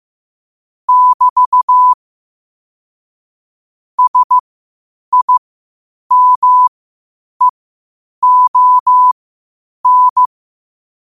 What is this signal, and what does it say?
Morse code, heard as a keyed tone, about 15 wpm, spaced as for 8 wpm, 1000 Hz. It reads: <BT> SIMEON
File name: qotdfiles / 8wpm